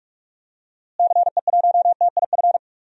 Morse code, heard as a keyed tone, about 46 wpm, 700 Hz. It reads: CE1TIF
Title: DXCC-100 Call Signs - 6 Characters - Repeated 2x - 45wpm